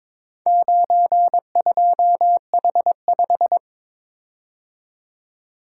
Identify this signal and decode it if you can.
Morse code, heard as a keyed tone, about 23 wpm, 700 Hz. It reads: 92H5
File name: practice / 22wpm / Four Letter-Number Combinations - 22wpm